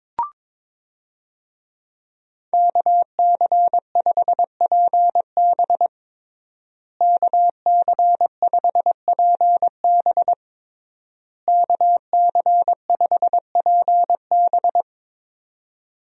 Morse code, heard as a keyed tone, about 22 wpm, 700 Hz. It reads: KC5PB KC5PB KC5PB
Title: USA Call Signs - 5 Characters - Repeated 3x - 22wpm